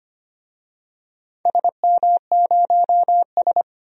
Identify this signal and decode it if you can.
Morse code, heard as a keyed tone, about 25 wpm, 700 Hz. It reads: SM0S
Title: DXCC-100 Call Signs - 4 Characters - 25wpm